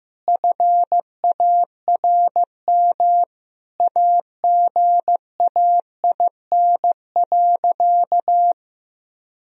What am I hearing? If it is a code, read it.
Morse code, heard as a keyed tone, about 15 wpm, 700 Hz. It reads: FARM AGAIN.
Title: Sentences from Top 1000 Words - Review Part 2 - 15wpm